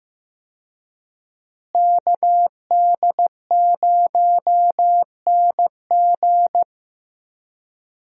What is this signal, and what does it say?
Morse code, heard as a keyed tone, about 16 wpm, 700 Hz. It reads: KD0NG
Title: USA Call Signs - 5 Characters - Repeated 3x - 15wpm